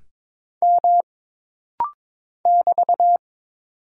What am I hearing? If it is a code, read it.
Morse code, heard as a keyed tone, about 22 wpm, 700 Hz. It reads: M <BT>
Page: This rapid-fire timing is geared to push your ICR (Instant Character Recognition) skills.